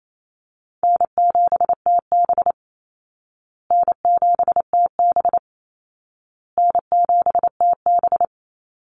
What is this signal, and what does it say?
Morse code, heard as a keyed tone, about 28 wpm, 700 Hz. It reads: N7TB N7TB N7TB
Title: Easy Call Signs Repeated 3x - 28wpm